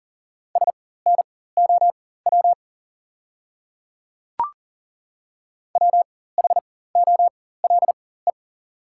Morse code, heard as a keyed tone, about 40 wpm, spaced as for 19 wpm, 690 Hz. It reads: SNOW WHOLE
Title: 4x Character Spacing - Top 500 Words - 40wpm